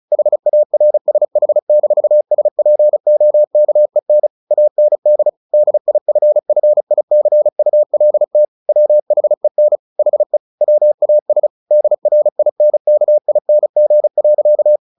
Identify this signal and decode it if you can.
Morse code, heard as a keyed tone, about 35 wpm, 600 Hz. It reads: HARSH-SPOKEN AND DIFFICULT WHEN HE WAS DRINKING.
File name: the old man and the sea / the old man and the sea 1 35wpm